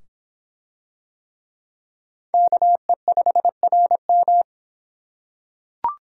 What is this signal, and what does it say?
Morse code, heard as a keyed tone, about 26 wpm, 700 Hz. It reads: KE5RM